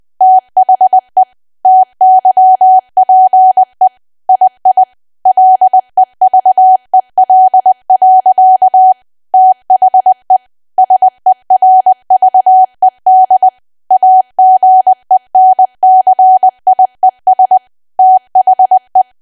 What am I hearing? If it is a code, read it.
Morse code, heard as a keyed tone, about 20 wpm, 750 Hz. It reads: THE TYPE II LEVEL. THE SERVED AGENCIES THE
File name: Archive / 20 WPM / 20WPM